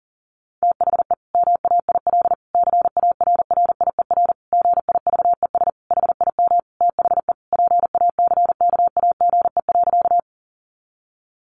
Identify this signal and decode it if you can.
Morse code, heard as a keyed tone, about 40 wpm, 700 Hz. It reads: THE MAIL CARRIER GIVES HIM THE PACKAGE.
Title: Additional Easy Phrases 40wpm